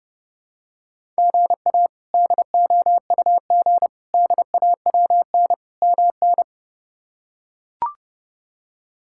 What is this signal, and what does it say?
Morse code, heard as a keyed tone, about 30 wpm, 700 Hz. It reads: GA DOUG DAWN MN
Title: K1USN SST QSO Practice - 30wpm